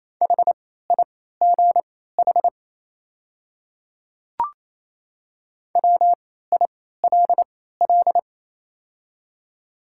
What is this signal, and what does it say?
Morse code, heard as a keyed tone, about 28 wpm, spaced as for 16 wpm, 700 Hz. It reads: HIGH WILL
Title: 3x Character Spacing - Top 200 Words - 28wpm